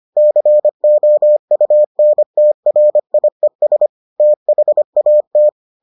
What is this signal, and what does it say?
Morse code, heard as a keyed tone, about 25 wpm, 600 Hz. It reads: COUNTRIES THAT